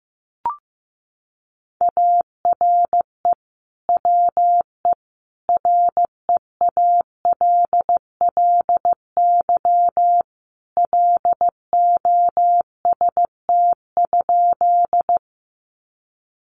Morse code, heard as a keyed tone, about 14 wpm, 700 Hz. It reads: ARE WE REALLY LOST?
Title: Sentences from Top 1400 Words 15wpm